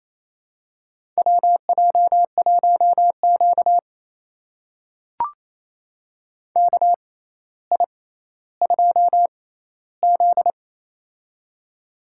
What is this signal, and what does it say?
Morse code, heard as a keyed tone, about 28 wpm, spaced as for 14 wpm, 700 Hz. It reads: WJ1Q KI2Z
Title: USA Call Signs - 4 Characters - 6x Character Spacing - 28wpm